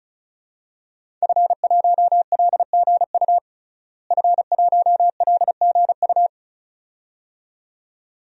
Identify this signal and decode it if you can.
Morse code, heard as a keyed tone, about 36 wpm, 700 Hz. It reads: F1LGU F1LGU
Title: DXCC-100 Call Signs - 5 Characters - Repeated 2x - 35wpm